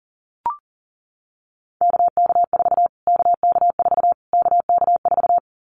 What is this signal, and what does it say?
Morse code, heard as a keyed tone, about 40 wpm, 700 Hz. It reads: KK4 KK4 KK4
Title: Callsign Prefixes for Top 25 DXCC - Repeated 3x - 40wpm